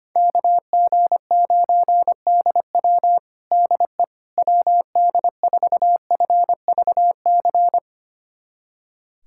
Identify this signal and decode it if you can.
Morse code, heard as a keyed tone, about 25 wpm, 700 Hz. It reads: KG9DW DE WD4FVC
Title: USA Callsign DE Callsign 25wpm